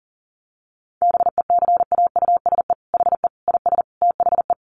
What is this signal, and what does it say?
Morse code, heard as a keyed tone, about 40 wpm, 700 Hz. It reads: BECAUSE HE IS THE
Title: Sentences from Top 1500 Words 40wpm